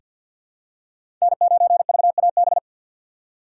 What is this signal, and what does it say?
Morse code, heard as a keyed tone, about 47 wpm, 700 Hz. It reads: N9VAB
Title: USA Call Signs - 5 Characters - 6x Character Spacing - 50wpm